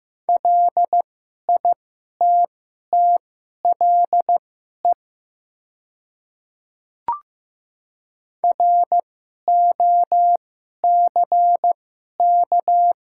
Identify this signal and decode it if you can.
Morse code, heard as a keyed tone, about 16 wpm, spaced as for 11 wpm, 700 Hz. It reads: LITTLE ROCK
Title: 2x Character Spacing - Top 500 Words - 15wpm